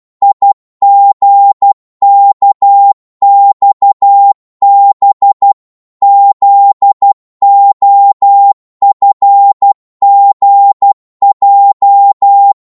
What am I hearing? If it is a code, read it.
Morse code, heard as a keyed tone, about 12 wpm, 800 Hz. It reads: IGKXB ZOFGJ
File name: Morse / ltr-12wpm